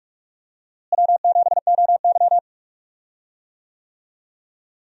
Morse code, heard as a keyed tone, about 46 wpm, 700 Hz. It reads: W7OY
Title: USA Call Signs - 4 Characters - Repeated 3x - 45wpm